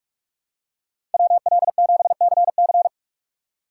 Morse code, heard as a keyed tone, about 43 wpm, 700 Hz. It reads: WL7CC
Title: USA Call Signs - 5 Characters - 6x Character Spacing - 45wpm